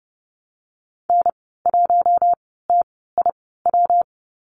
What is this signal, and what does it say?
Morse code, heard as a keyed tone, about 30 wpm, spaced as for 17 wpm, 700 Hz. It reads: N1TIW
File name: Lesson 19g - Letter w - Call Signs 30wpm